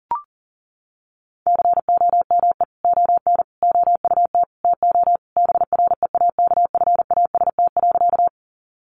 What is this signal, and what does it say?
Morse code, heard as a keyed tone, about 40 wpm, 700 Hz. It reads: COME ON OUT TO BREAKFAST.
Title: Sentences from Top 1500 Words 40wpm